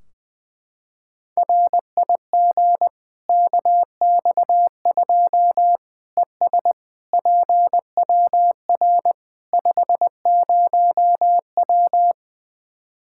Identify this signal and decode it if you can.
Morse code, heard as a keyed tone, about 20 wpm, 700 Hz. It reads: RIG KX2 ES PWR 50W
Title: Practice QSO - Exchange 2 - Info - 20wpm